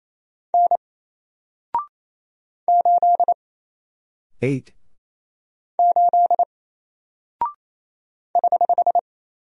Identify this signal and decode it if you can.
Morse code, heard as a keyed tone, about 27 wpm, 700 Hz. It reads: N 8 8 <HH>
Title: Single Character (Letter:Number:Punctuation:Prosign) 28wpm